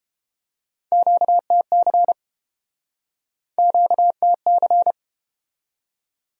Morse code, heard as a keyed tone, about 31 wpm, 700 Hz. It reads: QTC QTC